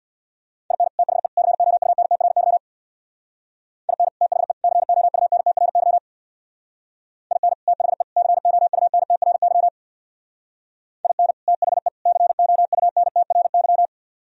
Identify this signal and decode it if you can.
Morse code, heard as a keyed tone, about 60 wpm, 700 Hz. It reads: IN THE COUNTRY IN THE COUNTRY IN THE COUNTRY IN THE COUNTRY
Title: Speed-Racing - Sets of 3 Words 50wpm